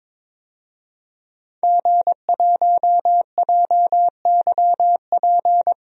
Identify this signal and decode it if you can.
Morse code, heard as a keyed tone, about 21 wpm, 700 Hz. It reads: G1JYP
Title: DXCC-100 Call Signs - 5 Characters - Repeated 3x - 22wpm